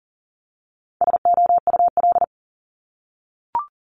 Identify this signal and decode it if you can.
Morse code, heard as a keyed tone, about 41 wpm, 700 Hz. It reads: SOUL
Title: Multi-letter Phonemes 40wpm